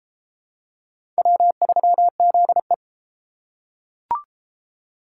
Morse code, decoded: W3ZE

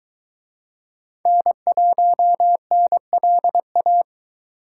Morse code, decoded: N1NLA

